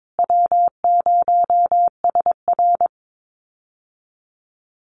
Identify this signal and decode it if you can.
Morse code, heard as a keyed tone, about 22 wpm, 700 Hz. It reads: W0SR
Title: USA Call Signs Repeated 3x 22wpm